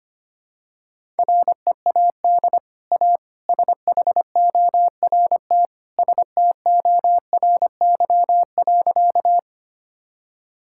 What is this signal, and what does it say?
Morse code, heard as a keyed tone, about 25 wpm, 690 Hz. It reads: READ A SHORT STORY.
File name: Easy Sentences with Hints - 25wpm